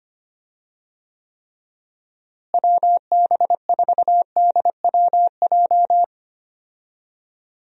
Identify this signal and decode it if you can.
Morse code, heard as a keyed tone, about 25 wpm, 700 Hz. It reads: WB4DWJ
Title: USA Call Signs - 6 Characters - Repeated 2x - 25wpm